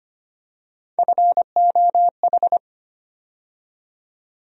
Morse code, heard as a keyed tone, about 25 wpm, 700 Hz. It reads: FOH
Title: USA Call Sign Postfixes - 3 Letters 25wpm